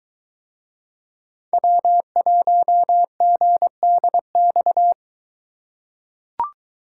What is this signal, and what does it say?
Morse code, heard as a keyed tone, about 22 wpm, 700 Hz. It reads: W1GDX